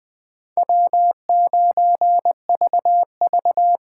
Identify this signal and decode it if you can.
Morse code, heard as a keyed tone, about 20 wpm, 700 Hz. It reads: W9VV
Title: Speed Racing - USA Call Signs 17wpm